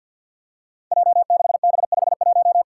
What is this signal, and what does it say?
Morse code, heard as a keyed tone, about 50 wpm, 700 Hz. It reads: J6B51
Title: Five Challenging Letter-Number Combinations 50wpm